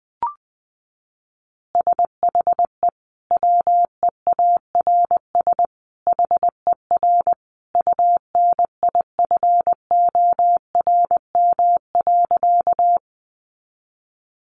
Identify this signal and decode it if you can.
Morse code, heard as a keyed tone, about 20 wpm, 700 Hz. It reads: SHE WEARS HER UNIFORM.
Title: Easy Phrases 20wpm